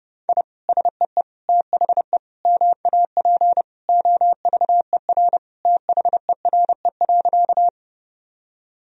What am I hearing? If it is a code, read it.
Morse code, heard as a keyed tone, about 30 wpm, 700 Hz. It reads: I SEE THE MAP OVER THERE.
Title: Sentences from Top 1000 Words - Review Part 2 - 30wpm